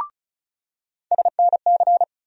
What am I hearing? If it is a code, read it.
Morse code, heard as a keyed tone, about 35 wpm, 700 Hz. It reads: SNC
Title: USA Call Sign Postfixes - 3 Letters 35wpm